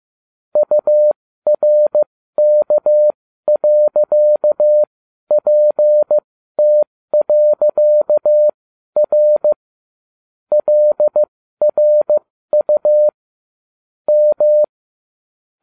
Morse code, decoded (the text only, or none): URK.PT.R LRU M